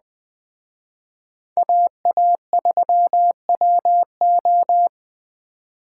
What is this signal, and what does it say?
Morse code, heard as a keyed tone, about 20 wpm, 700 Hz. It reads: AA3WO